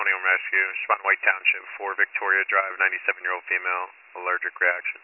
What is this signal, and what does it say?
Emergency Alert System (SAME) tones, heard as harsh buzzing bursts, with speech. It is heard from 1.0 to 4.8 s.